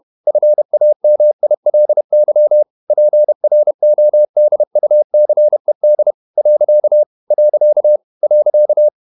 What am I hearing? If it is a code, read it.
Morse code, heard as a keyed tone, about 31 wpm, 600 Hz. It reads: FAMILY PRODUCED . .
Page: Never 31wpm | CW med Gnister